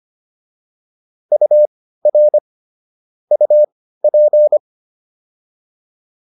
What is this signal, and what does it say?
Morse code, heard as a keyed tone, about 25 wpm, spaced as for 15 wpm, 600 Hz. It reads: UR UP